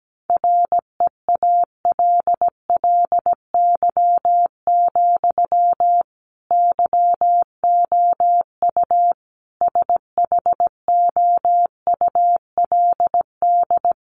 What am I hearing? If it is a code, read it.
Morse code, heard as a keyed tone, about 17 wpm, 700 Hz. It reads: REALLY, YOU SHOULD
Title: Sentences from Top 1400 Words 17wpm